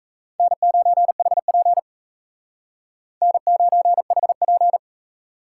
Morse code, heard as a keyed tone, about 33 wpm, 700 Hz. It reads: N9HP N9HP
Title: Speed Racing - USA Call Signs 28wpm